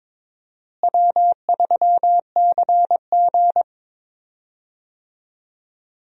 Morse code, heard as a keyed tone, about 22 wpm, 700 Hz. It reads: W3CG
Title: USA Call Signs - 4 Characters - Repeated 3x - 22wpm